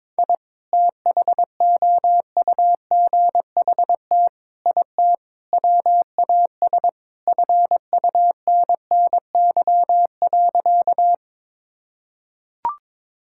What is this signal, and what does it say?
Morse code, heard as a keyed tone, about 21 wpm, 700 Hz. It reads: I THOUGHT IT WAS FUNNY.
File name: practice / 22wpm / Sentences with 4-Word Phrases - 22wpm